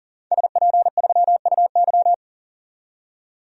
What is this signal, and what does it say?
Morse code, heard as a keyed tone, about 39 wpm, 700 Hz. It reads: SP3UY
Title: DXCC-100 Call Signs - 5 Characters - 40wpm